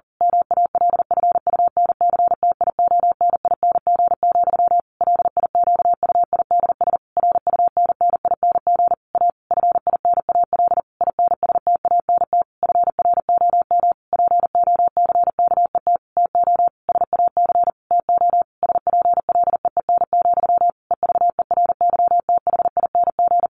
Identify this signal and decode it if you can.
Morse code, heard as a keyed tone, about 40 wpm, 700 Hz. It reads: MALFUNCTIONING, LIQUIDS RUNNING A FINAL INSTANT FROM POCKET TO SAC TO SPLEEN, EVERYTHING
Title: A Sound Of Thunder 40wpm